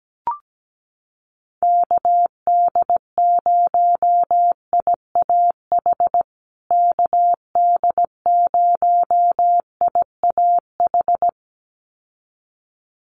Morse code, heard as a keyed tone, about 17 wpm, 700 Hz. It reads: KD0IAH KD0IAH
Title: USA Call Signs Repeated 2x 17wpm